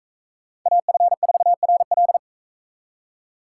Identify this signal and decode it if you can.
Morse code, heard as a keyed tone, about 41 wpm, 700 Hz. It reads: AF4RL